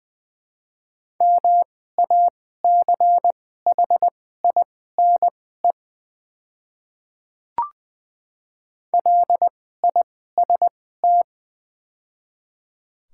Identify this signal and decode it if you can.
Morse code, heard as a keyed tone, about 20 wpm, spaced as for 14 wpm, 700 Hz. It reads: MACHINE LIST
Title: 2x Character Spacing - Top 500 Words - 20wpm